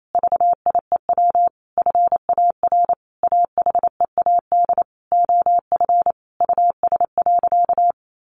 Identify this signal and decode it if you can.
Morse code, heard as a keyed tone, about 28 wpm, 700 Hz. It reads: VIEW FAR AHEAD OF US.
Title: Sentences from Top 1100 Words 28wpm